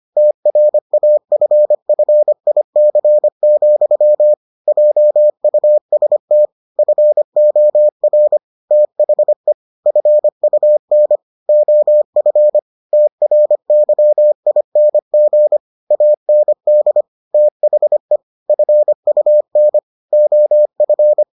Morse code, heard as a keyed tone, about 25 wpm, 600 Hz. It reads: TRAFFIC, JUST FOR THE FUN OF TRYING AND THE FUN OF